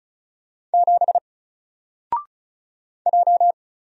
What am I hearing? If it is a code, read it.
Morse code, heard as a keyed tone, about 34 wpm, 700 Hz. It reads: 7 J
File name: Single-Letter-Number-35wpm